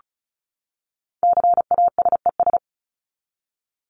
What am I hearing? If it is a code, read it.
Morse code, heard as a keyed tone, about 34 wpm, 700 Hz. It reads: CASES
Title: ES:S Suffix - Top 100 Words with Base Words 35wpm